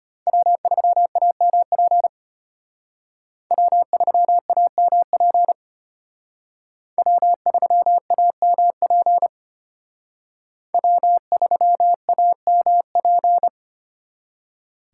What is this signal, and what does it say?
Morse code, heard as a keyed tone, about 33 wpm, 700 Hz. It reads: W3AMP W3AMP W3AMP W3AMP